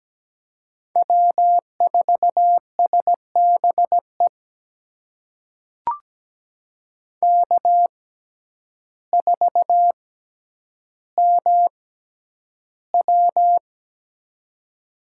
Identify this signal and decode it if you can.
Morse code, heard as a keyed tone, about 17 wpm, spaced as for 9 wpm, 700 Hz. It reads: W4SBE K4MW
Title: USA Call Signs - 6x Character Spacing - 17wpm